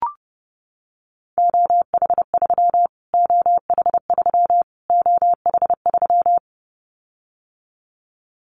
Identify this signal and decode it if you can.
Morse code, heard as a keyed tone, about 30 wpm, 700 Hz. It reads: OH3 OH3 OH3
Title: Callsign Prefixes for Top 25 DXCC - Repeated 3x - 30wpm